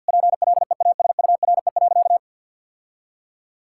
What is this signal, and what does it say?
Morse code, heard as a keyed tone, about 52 wpm, 700 Hz. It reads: PLEASURE.
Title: Sentences from Top 1600 Words 50wpm